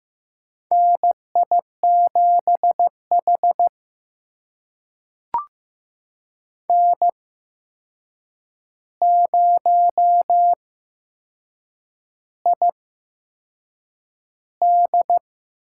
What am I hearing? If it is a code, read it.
Morse code, heard as a keyed tone, about 15 wpm, spaced as for 6 wpm, 700 Hz. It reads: NI7H N0ID
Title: USA Call Signs - 4 Characters - 8x Character Spacing - 15wpm